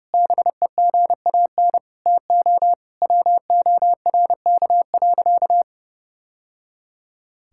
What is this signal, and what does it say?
Morse code, heard as a keyed tone, about 30 wpm, 700 Hz. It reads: BEGAN TO WORK.
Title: Sentences from Top 1000 Words - Review Part 6 - 30wpm